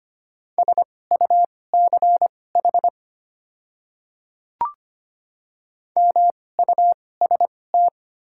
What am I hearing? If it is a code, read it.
Morse code, heard as a keyed tone, about 25 wpm, spaced as for 18 wpm, 690 Hz. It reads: SUCH MUST
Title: 2x Character Spacing - Top 500 Words - 25wpm